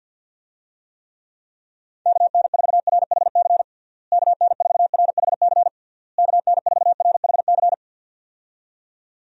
Morse code, heard as a keyed tone, about 48 wpm, 700 Hz. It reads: KN4RHC KN4RHC KN4RHC
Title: USA Call Signs - 6 Characters - Repeated 3x - 50wpm